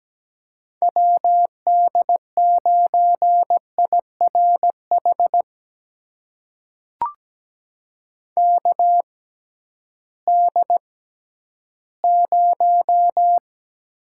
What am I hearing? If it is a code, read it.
Morse code, heard as a keyed tone, about 17 wpm, spaced as for 11 wpm, 700 Hz. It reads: WD9IRH KD0
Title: USA Call Signs - 6 Characters - 6x Character Spacing - 17wpm